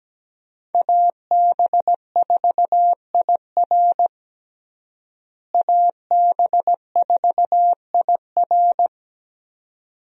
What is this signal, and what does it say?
Morse code, heard as a keyed tone, about 17 wpm, 700 Hz. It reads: AB4IR AB4IR
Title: USA Call Signs - 5 Characters - Repeated 2x - 17wpm